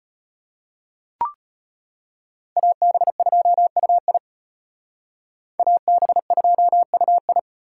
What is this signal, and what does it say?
Morse code, heard as a keyed tone, about 38 wpm, 700 Hz. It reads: AB2UI AB2UI